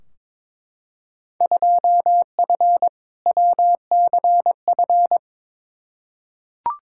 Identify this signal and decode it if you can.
Morse code, heard as a keyed tone, about 21 wpm, 700 Hz. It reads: 2F WCF
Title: Field Day Exchanges Encore - 22wpm